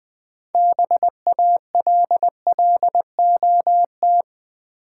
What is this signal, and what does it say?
Morse code, heard as a keyed tone, about 20 wpm, 700 Hz. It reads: BALLOT
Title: 6 Letter Words - First 3 Letters Given - 20wpm